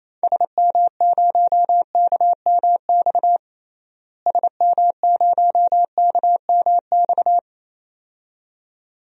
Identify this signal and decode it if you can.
Morse code, heard as a keyed tone, about 28 wpm, 700 Hz. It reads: SM0KMX SM0KMX
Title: DXCC-100 Call Signs - 6 Characters - Repeated 2x - 28wpm